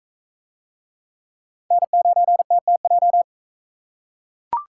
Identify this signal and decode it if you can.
Morse code, heard as a keyed tone, about 44 wpm, 700 Hz. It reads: N9TTJ